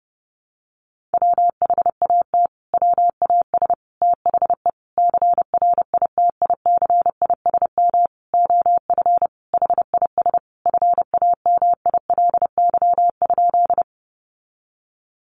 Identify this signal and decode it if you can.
Morse code, heard as a keyed tone, about 30 wpm, 700 Hz. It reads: WHAT WAS THE CRITICISM OF HIS FAMILY?
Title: Sentences from Top 1300 Words 30wpm